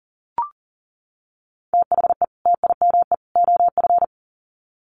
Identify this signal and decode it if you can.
Morse code, heard as a keyed tone, about 40 wpm, 700 Hz. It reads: THE TIME OF
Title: Sets of 3 Words 40wpm